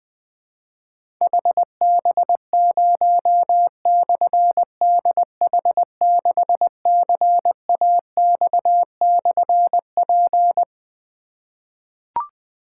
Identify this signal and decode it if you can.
Morse code, heard as a keyed tone, about 21 wpm, 700 Hz. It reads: HB0/DH6CAX/P
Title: Difficult Call Signs 20wpm